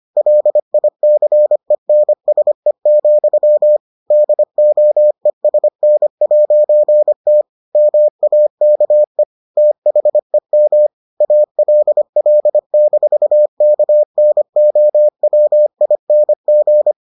Morse code, heard as a keyed tone, about 25 wpm, 600 Hz. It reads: LICENSE, DOESN'T MAKE THEM ALL-KNOWING